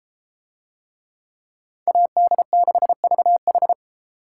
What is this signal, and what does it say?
Morse code, heard as a keyed tone, about 30 wpm, 700 Hz. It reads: AD6VH